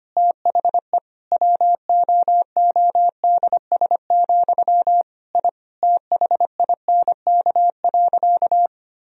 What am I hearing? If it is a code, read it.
Morse code, heard as a keyed tone, about 25 wpm, 700 Hz. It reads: THE WOODS, I THINK.
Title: Sentences from Top 1000 Words - Review Part 2 - 25wpm